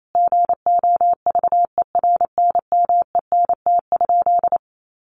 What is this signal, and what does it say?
Morse code, heard as a keyed tone, about 28 wpm, 700 Hz. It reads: GOVERNMENT?
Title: Sentences from Top 1400 Words 28wpm